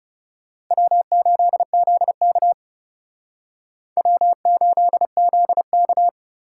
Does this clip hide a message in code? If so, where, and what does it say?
Yes: Morse code from 0.7 to 6.1 s, heard as a keyed tone, about 39 wpm, 700 Hz. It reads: W8ZK W8ZK